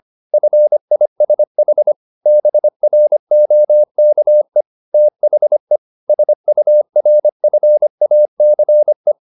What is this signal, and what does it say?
Morse code, heard as a keyed tone, about 25 wpm, 600 Hz. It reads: FISH BROKE THE SURFACE
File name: the old man and the sea part 1 25wpm